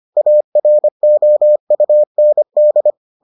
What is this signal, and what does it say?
Morse code, heard as a keyed tone, about 25 wpm, 600 Hz. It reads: AROUND